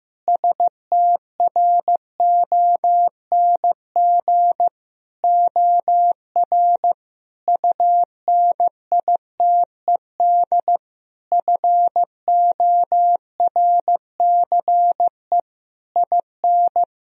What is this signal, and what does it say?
Morse code, heard as a keyed tone, about 15 wpm, 700 Hz. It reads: STRONG OR UNITED FORCE IN
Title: Sentences from Top 1000 Words - Review Part 4 - 15wpm